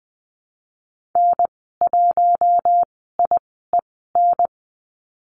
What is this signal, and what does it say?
Morse code, heard as a keyed tone, about 20 wpm, spaced as for 14 wpm, 700 Hz. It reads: N1IEN
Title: 042 Lesson 8g - Number 1 - Call Signs 20wpm